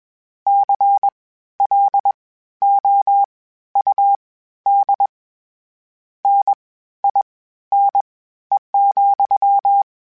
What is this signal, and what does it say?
Morse code, heard as a keyed tone, about 21 wpm, spaced as for 13 wpm, 800 Hz. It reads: CLOUD NINE,
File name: Misc / Misc-mp3s / WordPairs sp @20